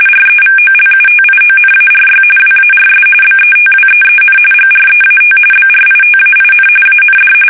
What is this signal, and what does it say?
RTTY, about 45.45 baud. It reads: REVOLUTIONARY WARS BETWEEN GREAT BRITAIN AND
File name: fsk850 / usnavy1